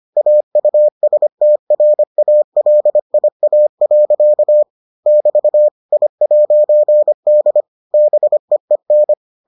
Morse code, heard as a keyed tone, about 25 wpm, 600 Hz. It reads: AUSTRALIA. <BT> I'D BEEN